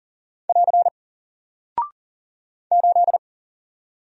Morse code, heard as a keyed tone, about 39 wpm, 700 Hz. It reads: <AR> 8
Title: Single Character (Letter:Number:Punctuation:Prosign) 40wpm